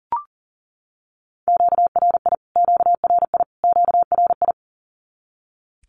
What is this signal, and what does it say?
Morse code, heard as a keyed tone, about 37 wpm, 700 Hz. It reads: QRI QRI QRI
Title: Q-Codes Repeated 3x 40wpm